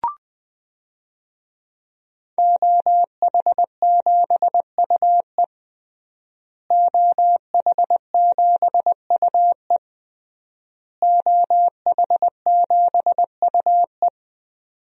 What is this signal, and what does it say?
Morse code, heard as a keyed tone, about 21 wpm, 700 Hz. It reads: OH7UE OH7UE OH7UE
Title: DXCC-100 Call Signs - 5 Characters - Repeated 3x - 20wpm